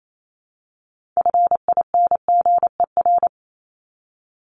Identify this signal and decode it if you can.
Morse code, heard as a keyed tone, about 28 wpm, 700 Hz. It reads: FINGER
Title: Top 1501-1750 Words 28wpm